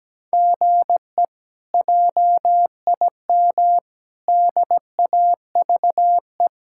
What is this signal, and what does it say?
Morse code, heard as a keyed tone, about 17 wpm, 700 Hz. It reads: GE JIM DAVE
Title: K1USN SST QSO Practice - 17wpm